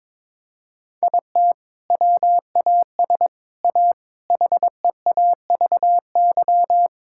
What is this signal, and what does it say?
Morse code, heard as a keyed tone, about 22 wpm, 700 Hz. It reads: IT WAS A HEAVY
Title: Sentences from Top 1600 Words 22wpm